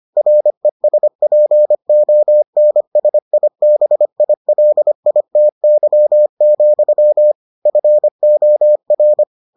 Morse code, heard as a keyed tone, about 25 wpm, 600 Hz. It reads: RESPONSIBILITY, FOR